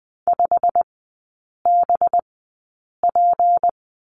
Morse code, heard as a keyed tone, about 20 wpm, spaced as for 8 wpm, 690 Hz. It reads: 5BP
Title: 212 Lesson 34e - Number 8 - Three Characters 20wpm